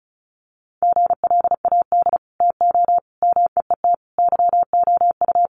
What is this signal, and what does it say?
Morse code, heard as a keyed tone, about 34 wpm, 700 Hz. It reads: GLAD TO MEET YOU
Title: Sentences from Top 1100 Words 35wpm